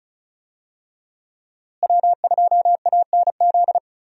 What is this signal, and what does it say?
Morse code, heard as a keyed tone, about 36 wpm, 700 Hz. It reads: W2ANZ
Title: USA Call Signs - 5 Characters - Repeated 3x - 35wpm